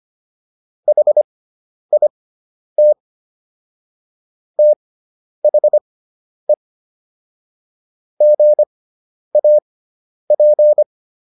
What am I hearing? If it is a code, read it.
Morse code, heard as a keyed tone, about 25 wpm, spaced as for 10 wpm, 600 Hz. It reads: HIT THE GAP